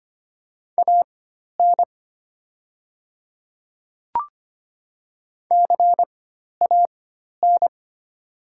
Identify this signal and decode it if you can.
Morse code, heard as a keyed tone, about 25 wpm, spaced as for 12 wpm, 690 Hz. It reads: AN CAN
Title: 4x Character Spacing - Top 100 Words - 25wpm